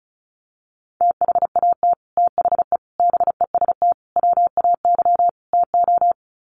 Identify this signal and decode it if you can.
Morse code, heard as a keyed tone, about 34 wpm, 700 Hz. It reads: THAT THE BEST WAY TO
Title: Sets of 5 Words Encore2 35wpm